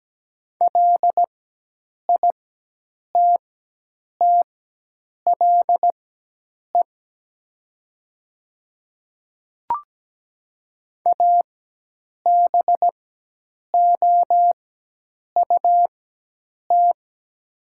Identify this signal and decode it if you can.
Morse code, heard as a keyed tone, about 17 wpm, spaced as for 8 wpm, 700 Hz. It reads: LITTLE ABOUT
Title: 4x Character Spacing - Top 200 Words - 17wpm